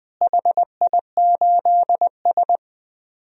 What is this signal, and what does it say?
Morse code, heard as a keyed tone, about 20 wpm, 700 Hz. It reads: HI8S
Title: DXCC-100 Call Signs - 4 Characters - Repeated 3x - 20wpm